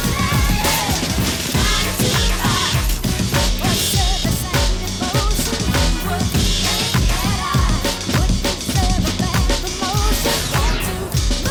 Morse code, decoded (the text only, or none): none